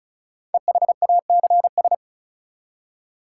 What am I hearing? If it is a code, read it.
Morse code, heard as a keyed tone, about 34 wpm, 700 Hz. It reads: EHACS
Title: Code Groups - 5 Letters - 35wpm